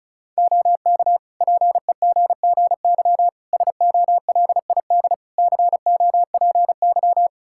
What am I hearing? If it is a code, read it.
Morse code, heard as a keyed tone, about 36 wpm, 700 Hz. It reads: OK PEGGY SOLID COPY
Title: Practice QSO - Exchange 3 - More Info - 35wpm